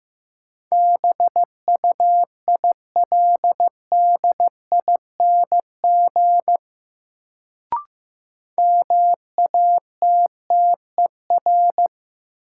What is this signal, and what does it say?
Morse code, heard as a keyed tone, about 15 wpm, 700 Hz. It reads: BUILDING MATTER
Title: Sentences from Top 700 Words - New Word Drill - 15wpm